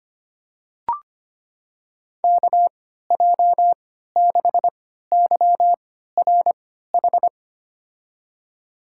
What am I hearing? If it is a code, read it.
Morse code, heard as a keyed tone, about 25 wpm, spaced as for 14 wpm, 700 Hz. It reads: KJ6YRH
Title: USA Call Signs - 6 Characters - 3x Character Spacing - 25wpm